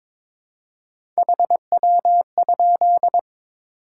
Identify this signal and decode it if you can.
Morse code, heard as a keyed tone, about 22 wpm, 700 Hz. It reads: HW?
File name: Top 20 QSO Elements - 3x Character Spacing - 22wpm